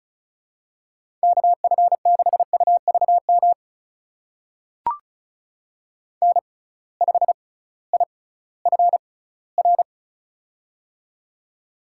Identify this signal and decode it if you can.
Morse code, heard as a keyed tone, about 36 wpm, spaced as for 19 wpm, 700 Hz. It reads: KF6UVM N5IFR
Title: USA Call Signs - 6x Character Spacing - 35wpm